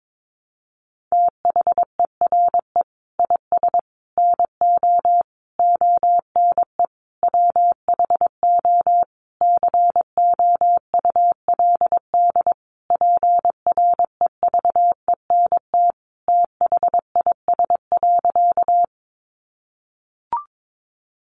Morse code, heard as a keyed tone, about 22 wpm, 700 Hz. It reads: THERE IS NO ONE WHO COULD PREVENT THIS.
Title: Sentences from Top 1100 Words 22wpm